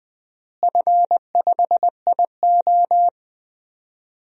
Morse code, heard as a keyed tone, about 21 wpm, 700 Hz. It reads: F5IO
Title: DXCC-100 Call Signs - 4 Characters - Repeated 3x - 20wpm